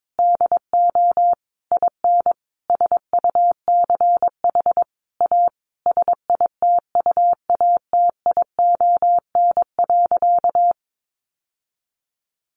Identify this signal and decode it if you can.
Morse code, heard as a keyed tone, about 22 wpm, 700 Hz. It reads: DO IN SUCH A SITUATION.
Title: Sentences from Top 1200 Words 22wpm